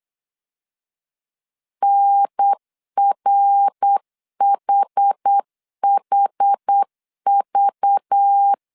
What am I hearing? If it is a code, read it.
Morse code, heard as a keyed tone, about 8 wpm, 800 Hz. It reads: NRHHV
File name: 8wpm